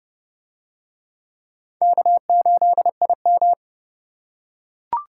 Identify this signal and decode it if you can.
Morse code, heard as a keyed tone, about 28 wpm, 700 Hz. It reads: K8IM